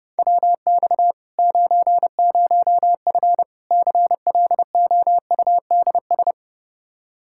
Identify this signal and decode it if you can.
Morse code, heard as a keyed tone, about 32 wpm, 700 Hz. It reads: WX 90F CLOUDS
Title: Practice QSO - Exchange 2 - Info - 30wpm